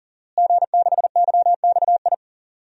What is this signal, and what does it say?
Morse code, heard as a keyed tone, about 40 wpm, 700 Hz. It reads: G6YXI
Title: DXCC-100 Call Signs - 5 Characters - Repeated 3x - 40wpm